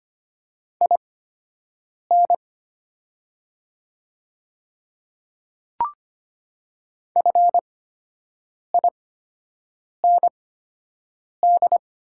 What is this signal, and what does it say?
Morse code, heard as a keyed tone, about 25 wpm, spaced as for 7 wpm, 700 Hz. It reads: IN FIND
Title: 8x Character Spacing - Top 200 Words - 25wpm